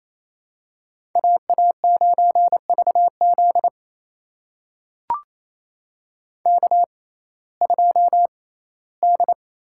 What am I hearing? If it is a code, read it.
Morse code, heard as a keyed tone, about 29 wpm, spaced as for 17 wpm, 700 Hz. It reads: AA9VZ K2D
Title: USA Call Signs - 6x Character Spacing - 28wpm